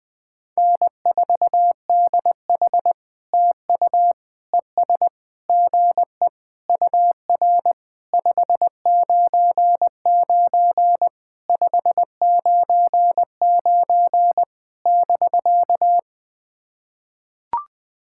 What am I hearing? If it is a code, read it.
Morse code, heard as a keyed tone, about 20 wpm, 700 Hz. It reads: N4DH TU ES GE UR 599 599 <BK>
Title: POTA QSO Practice - 20wpm